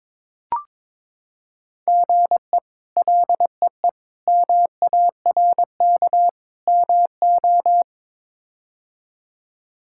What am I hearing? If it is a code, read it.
Morse code, heard as a keyed tone, about 22 wpm, 700 Hz. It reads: GE LEE MARK MO
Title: K1USN SST QSO Practice - 22wpm